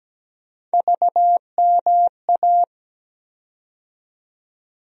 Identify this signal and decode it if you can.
Morse code, heard as a keyed tone, about 17 wpm, 700 Hz. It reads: VMA
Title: USA Call Sign Postfixes - 3 Letters 17wpm